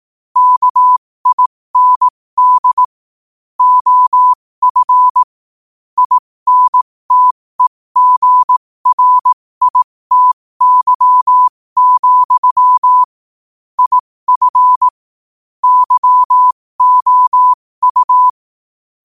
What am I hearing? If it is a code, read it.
Morse code, heard as a keyed tone, about 18 wpm, spaced as for 15 wpm, 1000 Hz. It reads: KIND OF INTEGRITY, IF YOU
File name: qotdfiles / 15wpm